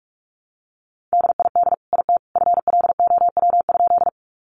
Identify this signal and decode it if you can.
Morse code, heard as a keyed tone, about 44 wpm, 700 Hz. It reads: DID IT FLOW?
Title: Sentences from Top 1200 Words 45wpm